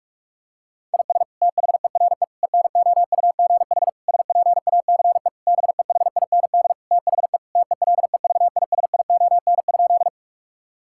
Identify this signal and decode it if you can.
Morse code, heard as a keyed tone, about 41 wpm, 700 Hz. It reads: IS THERE ENOUGH SPACE BEHIND THE TELEVISION?
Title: Sentences from Top 1000 Words - Review Part 5 - 45wpm